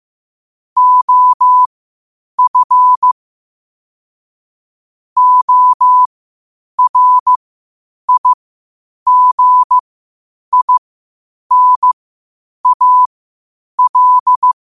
Morse code, heard as a keyed tone, about 15 wpm, spaced as for 8 wpm, 1000 Hz. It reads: OF ORIGINAL
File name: qotdfiles / 8wpm